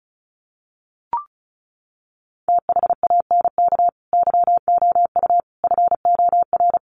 Morse code, heard as a keyed tone, about 34 wpm, 700 Hz. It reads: THANK YOU FOR
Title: Sets of 5 Words Encore 35wpm